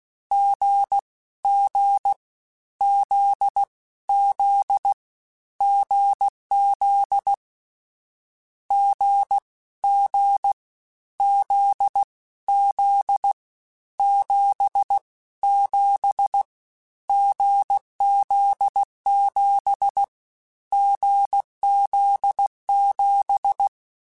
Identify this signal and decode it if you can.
Morse code, heard as a keyed tone, about 16 wpm, 800 Hz. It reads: G G Z Z GZ G G Z Z 7 7 GZ7 GZ7
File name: Difficult Combos / Difficult Combos-mp3s / awj1 ndb6 gz7 @15